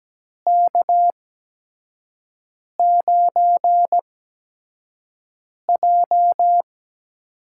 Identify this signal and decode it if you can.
Morse code, heard as a keyed tone, about 17 wpm, spaced as for 5 wpm, 700 Hz. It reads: K9J
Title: USA Call Signs - 4 Characters - 8x Character Spacing - 17wpm